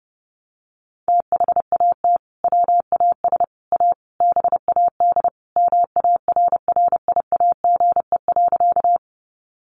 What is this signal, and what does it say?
Morse code, heard as a keyed tone, about 30 wpm, 700 Hz. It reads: THAT WAS A BAD MARRIAGE.
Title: Sentences from Top 800 Words 30wpm